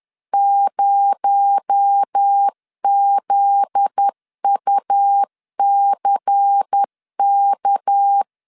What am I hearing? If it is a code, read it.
Morse code, heard as a keyed tone, about 10 wpm, 800 Hz. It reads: 0ZUCK
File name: LEZIONI / 7 GF9Z / 7 GF9Z 10wpm 2611